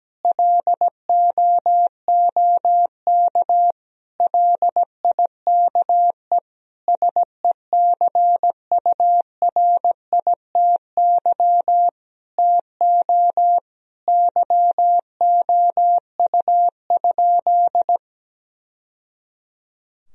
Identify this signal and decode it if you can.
Morse code, heard as a keyed tone, about 17 wpm, 700 Hz. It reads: LOOK LIKE SECURITY TO YOU?
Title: Sentences from Top 1000 Words - Review Part 4 - 17wpm